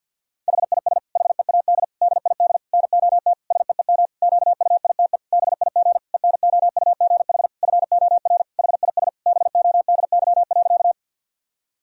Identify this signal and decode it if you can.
Morse code, heard as a keyed tone, about 53 wpm, 700 Hz. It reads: HIS HEAD DID NOT SEEM QUITE BIG ENOUGH FOR HIS BODY.
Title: Sentences from Top 1000 Words - Review Part 2 - 50wpm